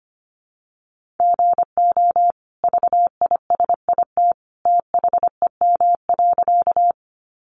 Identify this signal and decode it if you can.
Morse code, heard as a keyed tone, about 25 wpm, 700 Hz. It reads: GO VISIT THEM.
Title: Sentences from Top 900 Words 25wpm